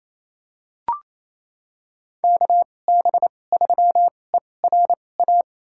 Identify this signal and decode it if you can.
Morse code, heard as a keyed tone, about 28 wpm, spaced as for 20 wpm, 700 Hz. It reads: KB3ERA
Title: USA Call Signs - 2x Character Spacing - 28wpm